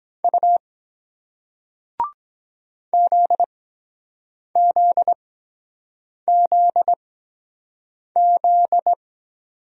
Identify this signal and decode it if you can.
Morse code, heard as a keyed tone, about 23 wpm, spaced as for 7 wpm, 700 Hz. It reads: U ZZZZ